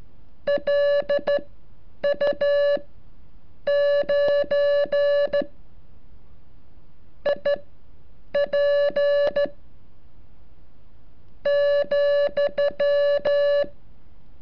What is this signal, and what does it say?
Morse code, heard as a keyed tone, about 13 wpm, spaced as for 6 wpm, 570 Hz. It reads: LU9IP,
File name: curso / ejercicio 36